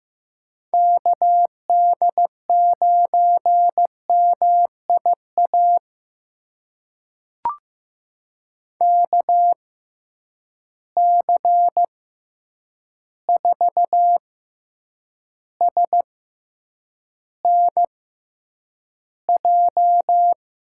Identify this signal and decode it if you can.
Morse code, heard as a keyed tone, about 15 wpm, spaced as for 8 wpm, 700 Hz. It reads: KD9MIA KC4SNJ
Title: USA Call Signs - 6 Characters - 6x Character Spacing - 15wpm